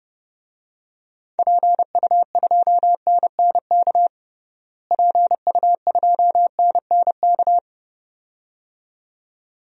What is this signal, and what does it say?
Morse code, heard as a keyed tone, about 32 wpm, 700 Hz. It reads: PU2NNK PU2NNK
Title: DXCC-100 Call Signs - 6 Characters - Repeated 2x - 30wpm